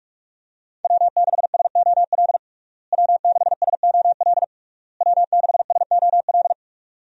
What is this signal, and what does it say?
Morse code, heard as a keyed tone, about 42 wpm, 700 Hz. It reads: W6SOL W6SOL W6SOL
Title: USA Call Signs - 5 Characters - Repeated 3x - 45wpm